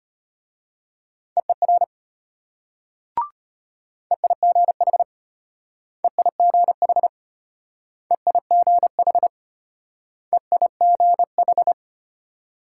Morse code, heard as a keyed tone, about 33 wpm, 700 Hz. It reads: EER EIGH EIGH EIGH EIGH